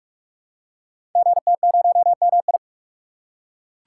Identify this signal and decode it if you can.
Morse code, heard as a keyed tone, about 44 wpm, 700 Hz. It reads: GT0MI
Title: Code Groups - 5 Letter:Numbers - 45wpm